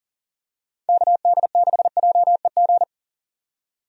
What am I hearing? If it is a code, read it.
Morse code, heard as a keyed tone, about 40 wpm, 700 Hz. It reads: KD6JEG
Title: USA Call Signs - 6 Characters - 40wpm